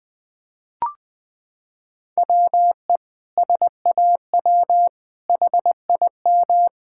Morse code, heard as a keyed tone, about 21 wpm, 700 Hz. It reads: WE SAW HIM
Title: Sentences from Top 1000 Words - Review Part 6 - 20wpm